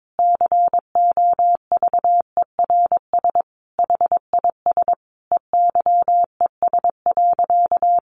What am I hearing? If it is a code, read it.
Morse code, heard as a keyed tone, about 22 wpm, 700 Hz. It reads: COVERS HIS EYES.
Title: Easy Phrases 22wpm